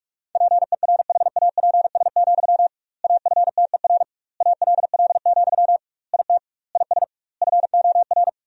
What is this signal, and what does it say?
Morse code, heard as a keyed tone, about 46 wpm, 700 Hz. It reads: PERHAPS, AFTER ALL, IT IS FOR
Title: 2x Word-Spacing - Sentences from Top 500 Words - 45wpm